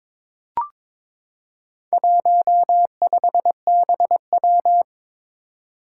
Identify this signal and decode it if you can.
Morse code, heard as a keyed tone, about 21 wpm, 700 Hz. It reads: 15BW
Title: Four Challenging Letter-Number Combinations 22wpm